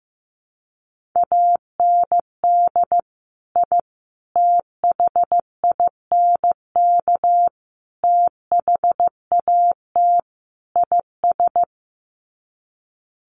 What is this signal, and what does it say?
Morse code, heard as a keyed tone, about 15 wpm, 700 Hz. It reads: AND I THINK THAT IS
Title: Sets of 5 Words Encore2 15wpm